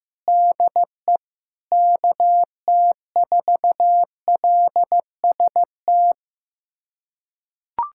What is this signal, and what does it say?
Morse code, heard as a keyed tone, about 15 wpm, 700 Hz. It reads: DE KT4LST
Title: USA Callsign DE Callsign 15wpm